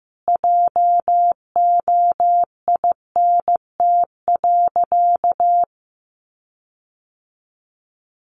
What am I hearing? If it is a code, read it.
Morse code, heard as a keyed tone, about 15 wpm, 700 Hz. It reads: JOINT.
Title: Sentences from Top 1300 Words 15wpm